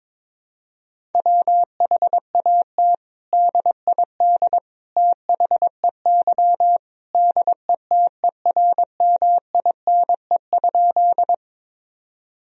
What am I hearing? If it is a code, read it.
Morse code, heard as a keyed tone, about 22 wpm, 700 Hz. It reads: WHAT DID THEY DETERMINE?
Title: Sentences from Top 1000 Words - Review Part 4 - 22wpm